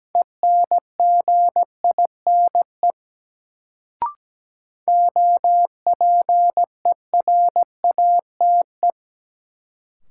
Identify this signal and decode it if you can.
Morse code, heard as a keyed tone, about 17 wpm, 700 Hz. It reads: ENGINE OPERATE
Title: Sentences from Top 1500 Words - New Word Drill - 17wpm